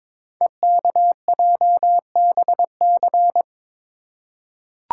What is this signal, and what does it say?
Morse code, heard as a keyed tone, about 22 wpm, 700 Hz. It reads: EKJBC
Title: Code Groups - 5 Letters - 22wpm